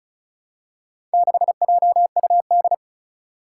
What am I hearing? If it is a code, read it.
Morse code, heard as a keyed tone, about 36 wpm, 700 Hz. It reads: 6JUD
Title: Four Challenging Letter-Number Combinations 35wpm